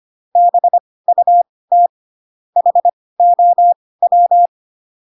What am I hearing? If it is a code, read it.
Morse code, heard as a keyed tone, about 25 wpm, spaced as for 18 wpm, 700 Hz. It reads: BUT HOW